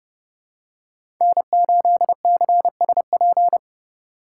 Morse code, heard as a keyed tone, about 32 wpm, 700 Hz. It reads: N8CSP
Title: USA Call Signs - 5 Characters - 2x Character Spacing - 30wpm